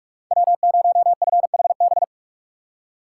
Morse code, heard as a keyed tone, about 42 wpm, 700 Hz. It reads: W0FHB
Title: USA Call Signs - 5 Characters - 3x Character Spacing - 45wpm